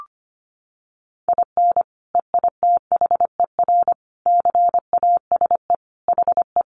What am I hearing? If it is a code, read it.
Morse code, heard as a keyed tone, about 25 wpm, 700 Hz. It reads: IN EITHER CASE HE
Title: Sentences from Top 1300 Words 25wpm